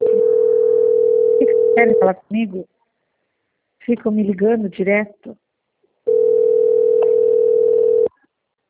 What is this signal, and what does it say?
Ringback tone (North America), 440 + 480 Hz, with speech.
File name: out- - - - - .3574